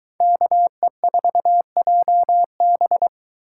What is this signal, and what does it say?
Morse code, heard as a keyed tone, about 22 wpm, 700 Hz. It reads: KE4JB